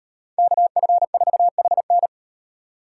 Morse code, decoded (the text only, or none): KF4HN